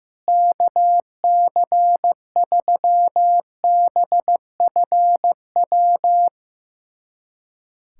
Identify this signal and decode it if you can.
Morse code, heard as a keyed tone, about 15 wpm, 700 Hz. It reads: KC3BFW
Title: USA Callsign DE Callsign 15wpm